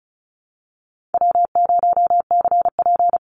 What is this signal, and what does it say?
Morse code, heard as a keyed tone, about 34 wpm, 700 Hz. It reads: W0CP
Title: NA SOTA Chaser Callsigns 35wpm